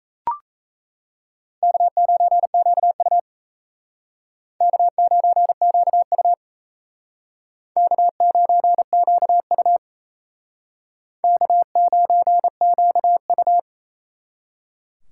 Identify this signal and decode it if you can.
Morse code, heard as a keyed tone, about 36 wpm, 700 Hz. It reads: K9QU K9QU K9QU K9QU